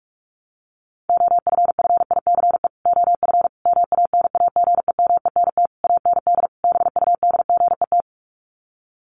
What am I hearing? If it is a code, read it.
Morse code, heard as a keyed tone, about 44 wpm, 700 Hz. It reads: OFFICE OF MANAGEMENT AND BUDGET
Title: Sets of 5 Words Encore2 45wpm